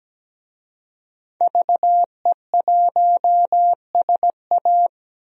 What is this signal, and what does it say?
Morse code, heard as a keyed tone, about 17 wpm, 700 Hz. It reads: VE1SA
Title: DXCC-100 Call Signs - 5 Characters - 17wpm